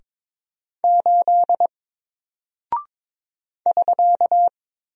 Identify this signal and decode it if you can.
Morse code, heard as a keyed tone, about 22 wpm, 700 Hz. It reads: 8 <SK>
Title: Single Character (Letter:Number:Punctuation:Prosign) Alt 22wpm